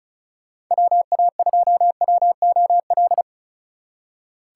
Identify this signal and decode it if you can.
Morse code, heard as a keyed tone, about 44 wpm, 700 Hz. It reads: WA2WOL
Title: Speed Racing - USA 2x3 Call Signs 35wpm